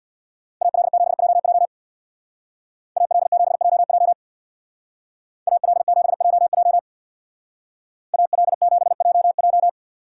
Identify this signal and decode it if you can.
Morse code, heard as a keyed tone, about 65 wpm, 700 Hz. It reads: AL7JJ AL7JJ AL7JJ AL7JJ